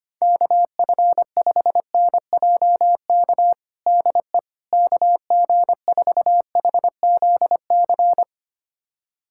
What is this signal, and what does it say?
Morse code, heard as a keyed tone, about 25 wpm, 700 Hz. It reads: KF5NJK DE KG4HZC
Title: USA Callsign DE Callsign - 6 Characters - 25wpm